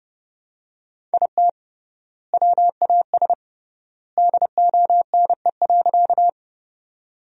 Morse code, heard as a keyed tone, about 32 wpm, 700 Hz. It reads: IT WAS DONE.
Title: 3x Word-Spacing - Sentences from Top 500 Words - 30wpm